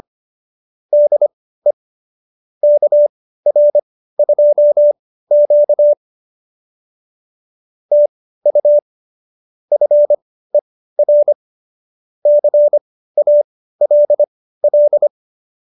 Morse code, heard as a keyed tone, about 25 wpm, spaced as for 15 wpm, 590 Hz. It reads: DE KR2Q TU FER CALL